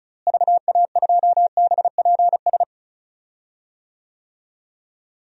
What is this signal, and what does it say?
Morse code, heard as a keyed tone, about 36 wpm, 700 Hz. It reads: VA2BPS
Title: DXCC-100 Call Signs - 6 Characters - Repeated 2x - 35wpm